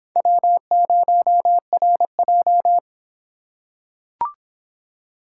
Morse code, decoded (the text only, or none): W0RJ